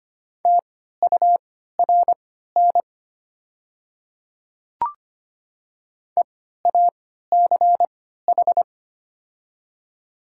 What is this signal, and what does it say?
Morse code, heard as a keyed tone, about 25 wpm, spaced as for 14 wpm, 700 Hz. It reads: TURN EACH
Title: 3x Character Spacing - Top 200 Words - 25wpm